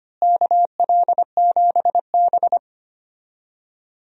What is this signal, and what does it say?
Morse code, heard as a keyed tone, about 25 wpm, 700 Hz. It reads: KL7B
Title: DXCC-100 Call Signs - 4 Characters - Repeated 3x - 25wpm